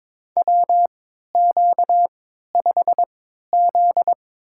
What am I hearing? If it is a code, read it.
Morse code, heard as a keyed tone, about 22 wpm, spaced as for 13 wpm, 700 Hz. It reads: WQ5Z
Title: USA Call Signs - 3x Character Spacing - 22wpm